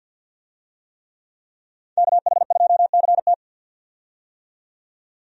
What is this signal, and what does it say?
Morse code, heard as a keyed tone, about 52 wpm, 700 Hz. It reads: KH2CT
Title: USA Call Signs - 5 Characters - 50wpm